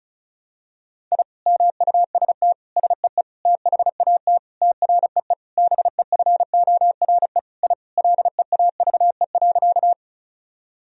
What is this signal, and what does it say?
Morse code, heard as a keyed tone, about 34 wpm, 700 Hz. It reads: I MUST SEE THAT TREE BEFORE I LEAVE.
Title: Sentences from Top 1000 Words - Review Part 1 - 35wpm